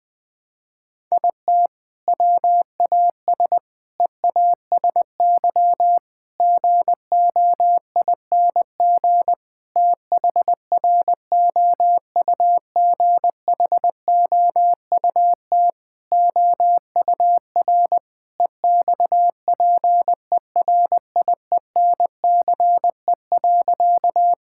Morse code, heard as a keyed tone, about 20 wpm, 700 Hz. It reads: IT WAS EASY GOING THROUGHOUT OUR EXPERIENCE.
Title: Sentences from Top 1000 Words - Review Part 5 - 20wpm